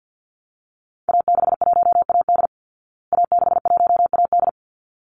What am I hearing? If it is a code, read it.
Morse code, heard as a keyed tone, about 48 wpm, 700 Hz. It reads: A61AD A61AD
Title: Call Signs Repeated 3x - 50wpm